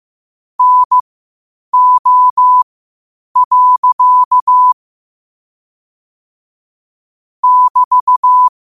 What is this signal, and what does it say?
Morse code, heard as a keyed tone, about 15 wpm, spaced as for 8 wpm, 1000 Hz. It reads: NO. <BT>
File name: qotdfiles / 8wpm